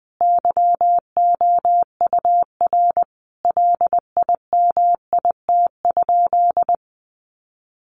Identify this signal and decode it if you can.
Morse code, heard as a keyed tone, about 20 wpm, 700 Hz. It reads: YOUR LIMIT?
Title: Sentences from Top 1500 Words 20wpm